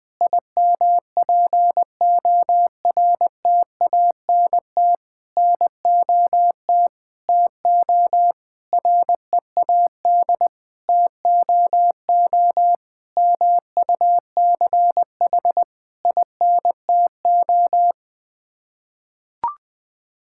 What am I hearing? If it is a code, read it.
Morse code, heard as a keyed tone, about 20 wpm, 700 Hz. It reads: IMPORTANT NOT TO READ TOO MUCH INTO
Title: Sets of 9 Words - 20wpm